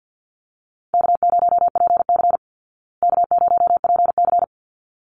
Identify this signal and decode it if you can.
Morse code, heard as a keyed tone, about 49 wpm, 700 Hz. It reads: K0PC K0PC
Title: Easy Call Signs Repeated 2x - 50wpm